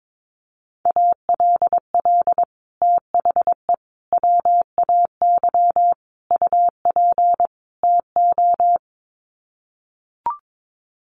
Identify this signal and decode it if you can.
Morse code, heard as a keyed tone, about 22 wpm, 700 Hz. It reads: ALL THE WAY UP TO
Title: Sets of 5 Words Encore 22wpm